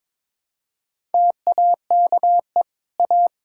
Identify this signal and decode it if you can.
Morse code, heard as a keyed tone, about 22 wpm, 700 Hz. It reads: TAKE A
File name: practice / 22wpm / Sets of 7 Words - 22wpm